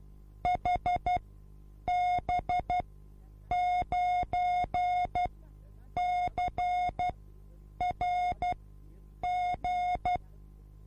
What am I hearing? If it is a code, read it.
Morse code, heard as a keyed tone, about 12 wpm, spaced as for 8 wpm, 720 Hz. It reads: HB9CRG
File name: repos / Morsekurs / DJ4UF Morsen24